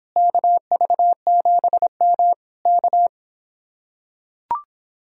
Morse code, heard as a keyed tone, about 26 wpm, 700 Hz. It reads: KV7M K